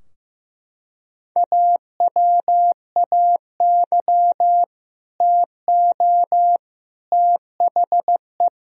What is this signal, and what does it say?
Morse code, heard as a keyed tone, about 15 wpm, 700 Hz. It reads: AWAY TO THE
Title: Sentences from Top 1000 Words - Review Part 2 - 15wpm